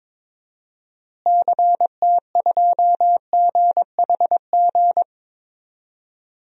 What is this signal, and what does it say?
Morse code, heard as a keyed tone, about 22 wpm, 700 Hz. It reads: CT2GHG
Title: DXCC-100 Call Signs - 6 Characters - Repeated 3x - 22wpm